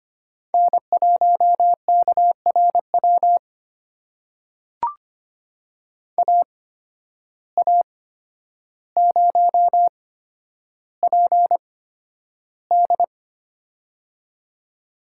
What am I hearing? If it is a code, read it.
Morse code, heard as a keyed tone, about 25 wpm, spaced as for 11 wpm, 700 Hz. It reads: N1KRW AA0PD
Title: USA Call Signs - 5 Characters - 8x Character Spacing - 25wpm